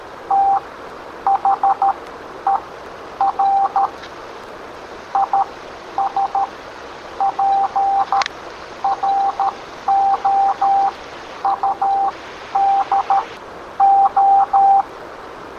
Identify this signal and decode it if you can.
Morse code, heard as a keyed tone, about 13 wpm, spaced as for 9 wpm, 790 Hz. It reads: THER IS PROUD O